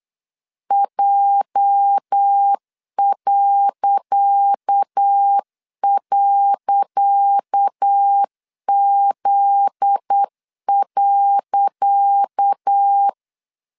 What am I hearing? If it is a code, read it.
Morse code, heard as a keyed tone, about 8 wpm, 800 Hz. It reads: J..Z.
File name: LEZIONI / 10 SIGNS / 10C DOT 8wpm 3649